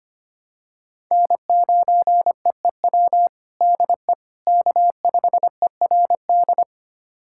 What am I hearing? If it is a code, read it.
Morse code, heard as a keyed tone, about 25 wpm, 700 Hz. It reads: N9EEW DE K5ERD
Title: USA Callsign DE Callsign - 5 Characters - 25wpm